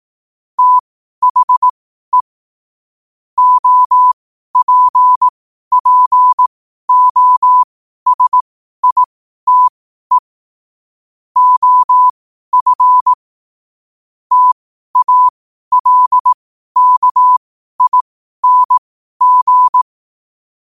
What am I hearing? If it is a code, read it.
Morse code, heard as a keyed tone, about 18 wpm, spaced as for 12 wpm, 1000 Hz. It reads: THE OPPOSITE OF TALKING
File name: qotdfiles / 12wpm